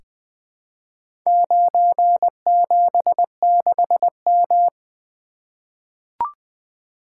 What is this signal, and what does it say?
Morse code, heard as a keyed tone, about 20 wpm, 700 Hz. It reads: 976M
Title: Four Letter-Number Combinations - 20wpm